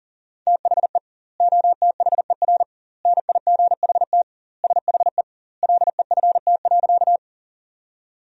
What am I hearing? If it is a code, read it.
Morse code, heard as a keyed tone, about 37 wpm, 700 Hz. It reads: THE OTHER NIGHT SHE LEFT.
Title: 2x Word-Spacing - Sentences from Top 300 Words - 40wpm